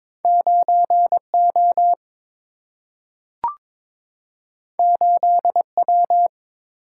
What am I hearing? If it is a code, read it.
Morse code, heard as a keyed tone, about 22 wpm, 700 Hz. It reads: 9O 8W
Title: Single-Letter Single-Number 22wpm